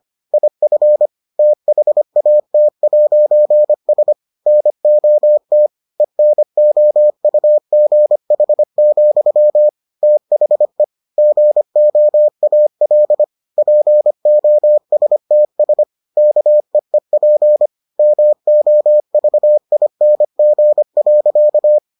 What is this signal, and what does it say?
Morse code, heard as a keyed tone, about 25 wpm, 600 Hz. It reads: IF THAT'S NOT ENOUGH, THE GOAL POSTS KEEP MOVING.